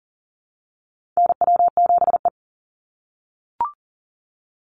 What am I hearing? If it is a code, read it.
Morse code, heard as a keyed tone, about 40 wpm, 700 Hz. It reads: NW7E
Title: NA SOTA Chaser Callsigns 40wpm